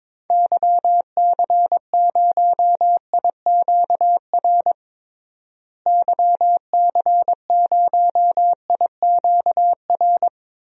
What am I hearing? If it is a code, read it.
Morse code, heard as a keyed tone, about 21 wpm, 700 Hz. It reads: YC0IQR YC0IQR
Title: DXCC-100 Call Signs - 6 Characters - Repeated 2x - 22wpm